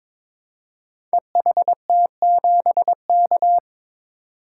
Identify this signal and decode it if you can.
Morse code, heard as a keyed tone, about 23 wpm, 700 Hz. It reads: EHT7K
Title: Code Groups - 5 Letter:Numbers - 22wpm